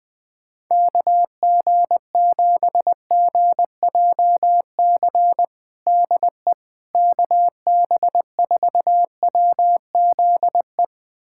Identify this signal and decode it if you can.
Morse code, heard as a keyed tone, about 20 wpm, 700 Hz. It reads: KG7GJC DE KB4WZE
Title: USA Callsign DE Callsign 20wpm